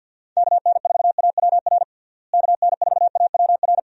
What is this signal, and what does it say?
Morse code, heard as a keyed tone, about 53 wpm, 700 Hz. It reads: KN4AWR KN4AWR
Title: USA Call Signs - 6 Characters - Repeated 2x - 50wpm